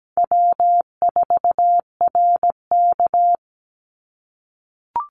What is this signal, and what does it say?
Morse code, heard as a keyed tone, about 17 wpm, 700 Hz. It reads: W4RK
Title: Easy Call Signs - 17wpm